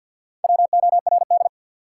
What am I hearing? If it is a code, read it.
Morse code, heard as a keyed tone, about 53 wpm, 700 Hz. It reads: WORD
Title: Sentences from Top 300 Words - New Word Drill - 50wpm